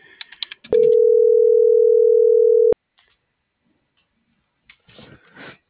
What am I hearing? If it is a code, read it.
Ringback tone (North America), 440 + 480 Hz.
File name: CENTRAL FRALSE / out brendaaviles.j